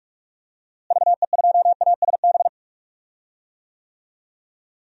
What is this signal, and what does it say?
Morse code, heard as a keyed tone, about 41 wpm, 700 Hz. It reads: VE2ASB
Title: DXCC-100 Call Signs - 6 Characters - Repeated 3x - 45wpm